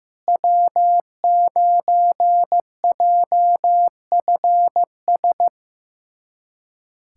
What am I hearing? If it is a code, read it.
Morse code, heard as a keyed tone, about 15 wpm, 700 Hz. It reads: W9JFS
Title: USA Callsign DE Callsign - 5 Characters - 15wpm